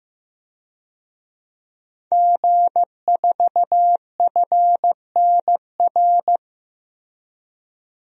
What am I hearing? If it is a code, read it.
Morse code, heard as a keyed tone, about 15 wpm, 700 Hz. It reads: G4FNR
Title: DXCC-100 Call Signs - 5 Characters - 15wpm